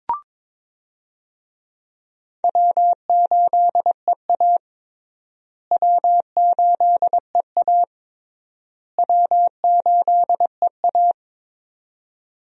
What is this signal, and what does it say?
Morse code, heard as a keyed tone, about 21 wpm, 700 Hz. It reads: W8EA W8EA W8EA
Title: USA Call Signs - 4 Characters - Repeated 3x - 22wpm